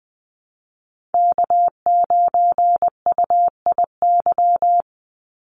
Morse code, heard as a keyed tone, about 19 wpm, 700 Hz. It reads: K9UIY
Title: Call Signs - 20wpm